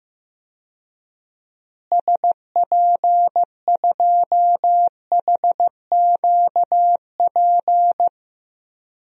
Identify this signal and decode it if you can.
Morse code, heard as a keyed tone, about 15 wpm, 700 Hz. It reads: SP2HQP
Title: DXCC-100 Call Signs - 6 Characters - Repeated 3x - 15wpm